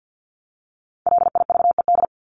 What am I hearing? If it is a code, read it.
Morse code, heard as a keyed tone, about 50 wpm, 700 Hz. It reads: LIVED
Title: ED Suffix - Top 100 Words with Base Words 50wpm